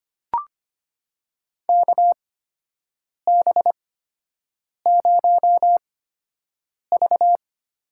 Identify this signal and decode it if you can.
Morse code, heard as a keyed tone, about 25 wpm, spaced as for 7 wpm, 700 Hz. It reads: KB0V
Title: USA Call Signs - 8x Character Spacing - 25wpm